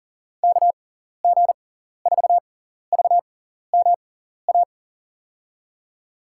Morse code, heard as a keyed tone, about 40 wpm, spaced as for 14 wpm, 700 Hz. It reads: KG4VMA
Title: USA Call Signs - 6 Characters - 6x Character Spacing - 40wpm